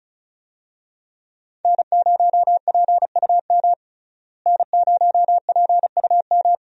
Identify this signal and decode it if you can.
Morse code, heard as a keyed tone, about 36 wpm, 700 Hz. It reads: N0PUM N0PUM
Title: USA Call Signs - 5 Characters - Repeated 2x - 35wpm